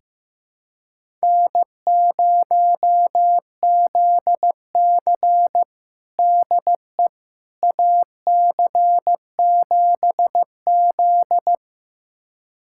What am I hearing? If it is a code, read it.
Morse code, heard as a keyed tone, about 15 wpm, 700 Hz. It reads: N0ZC DE AC7Z
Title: USA Callsign DE Callsign - 4 Characters - 15wpm